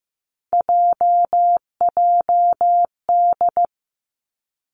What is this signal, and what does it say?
Morse code, heard as a keyed tone, about 15 wpm, 700 Hz. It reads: JJD
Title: Three Challenging Letter-Number Combinations 15wpm